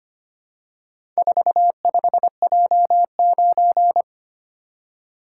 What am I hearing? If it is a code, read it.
Morse code, heard as a keyed tone, about 25 wpm, 700 Hz. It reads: 45J9
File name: practice / 25wpm / Four Letter-Number Combinations - 25wpm